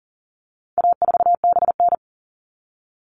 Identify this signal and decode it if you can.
Morse code, heard as a keyed tone, about 40 wpm, 700 Hz. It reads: A4BN
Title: All Cut Numbers - 4-Digit 40wpm